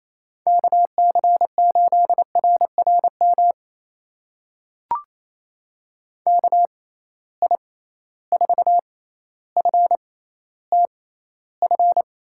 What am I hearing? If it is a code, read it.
Morse code, heard as a keyed tone, about 28 wpm, spaced as for 14 wpm, 700 Hz. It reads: KC8RRM KI4FTF
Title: USA Call Signs - 6 Characters - 6x Character Spacing - 28wpm